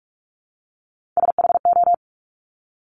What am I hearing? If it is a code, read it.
Morse code, heard as a keyed tone, about 46 wpm, 690 Hz. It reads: SHO
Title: Top Three Letter Combinations in Words 45wpm